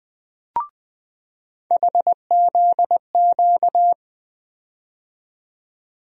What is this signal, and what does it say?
Morse code, heard as a keyed tone, about 20 wpm, 700 Hz. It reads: HZQ
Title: USA Call Sign Postfixes - 3 Letters 20wpm